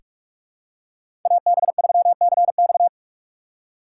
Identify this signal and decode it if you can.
Morse code, heard as a keyed tone, about 46 wpm, 700 Hz. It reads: AB3CX
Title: USA Call Signs - 4x Character Spacing - 45wpm